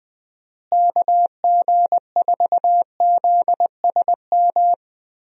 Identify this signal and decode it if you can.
Morse code, heard as a keyed tone, about 20 wpm, 700 Hz. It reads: KG4ZSM
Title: USA Call Signs - 6 Characters - Repeated 2x - 20wpm